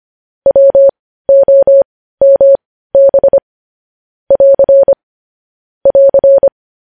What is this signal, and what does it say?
Morse code, heard as a keyed tone, about 25 wpm, spaced as for 15 wpm, 550 Hz. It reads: WOMB <AR> <AR>
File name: podcast / News headlines-1x-25x15